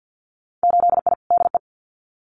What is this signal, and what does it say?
Morse code, heard as a keyed tone, about 50 wpm, 700 Hz. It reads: 8I DE
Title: Winter Field Day Exchanges Encore 50wpm